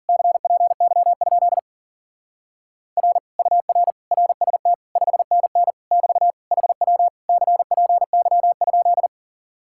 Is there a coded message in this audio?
Yes: Morse code from 0.1 to 9.1 s, heard as a keyed tone, about 51 wpm, 700 Hz. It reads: CPY? R UR RST 5NN <BT> HW CPY?